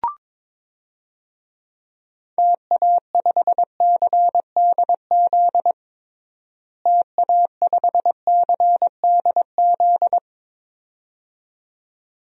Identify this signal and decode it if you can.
Morse code, heard as a keyed tone, about 22 wpm, 700 Hz. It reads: TA5CDZ TA5CDZ
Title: DXCC-100 Call Signs - 6 Characters - Repeated 2x - 22wpm